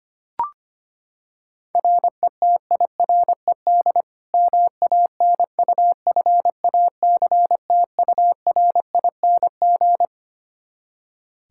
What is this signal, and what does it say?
Morse code, heard as a keyed tone, about 25 wpm, 700 Hz. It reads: RETIRED MANUFACTURING
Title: Practice QSO - Exchange 3 - More Info - 25wpm